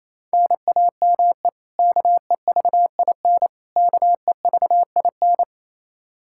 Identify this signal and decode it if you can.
Morse code, heard as a keyed tone, about 28 wpm, 700 Hz. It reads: NAME KEVIN KEVIN
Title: Practice QSO - Exchange 1 - Beginning - 28wpm